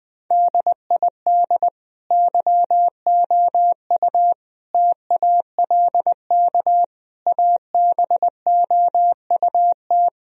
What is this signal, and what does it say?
Morse code, heard as a keyed tone, about 20 wpm, 700 Hz. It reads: DID YOU TALK ABOUT
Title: Sentences from Top 1000 Words - Review Part 4 - 20wpm